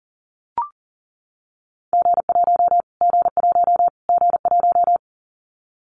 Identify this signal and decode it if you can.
Morse code, heard as a keyed tone, about 40 wpm, 700 Hz. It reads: G1 G1 G1
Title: Callsign Prefixes for Top 25 DXCC - Repeated 3x - 40wpm